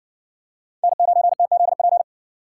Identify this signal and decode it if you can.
Morse code, heard as a keyed tone, about 58 wpm, 700 Hz. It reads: N9TZP